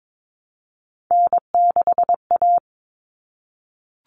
Morse code, heard as a keyed tone, about 22 wpm, 700 Hz. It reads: N6A
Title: All Cut Numbers - 3-Digit 22wpm